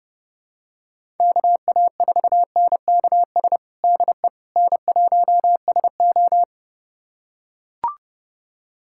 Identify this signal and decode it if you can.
Morse code, heard as a keyed tone, about 30 wpm, 700 Hz. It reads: KA4NKS DE N1SO
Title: USA Callsign DE Callsign 30wpm